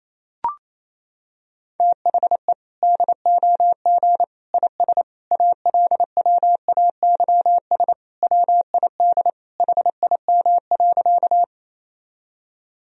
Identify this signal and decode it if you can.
Morse code, heard as a keyed tone, about 27 wpm, 700 Hz. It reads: THE DOG IS ALWAYS WID HIM.
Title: Sentences from Top 400 Words and Abbreviations 28wpm